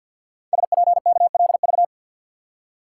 Morse code, heard as a keyed tone, about 53 wpm, 700 Hz. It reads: SPKLV
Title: Code Groups - 5 Letters - 50wpm